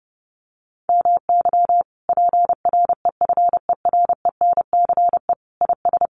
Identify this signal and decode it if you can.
Morse code, heard as a keyed tone, about 30 wpm, 700 Hz. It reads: MY PREFERENCE IS
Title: Sentences from Top 1500 Words 30wpm